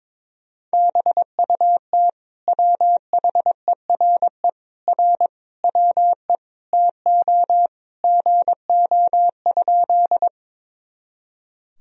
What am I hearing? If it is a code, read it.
Morse code, heard as a keyed tone, about 22 wpm, 700 Hz. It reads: BUT WHERE R WE TO GO?
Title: Sentences from Top 200 Words and Abbreviations 22wpm